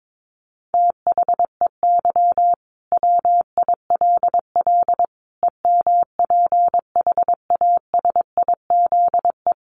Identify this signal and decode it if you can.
Morse code, heard as a keyed tone, about 22 wpm, 700 Hz. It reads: THEY WILL EMPHASIZE
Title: Sentences from Top 1500 Words 22wpm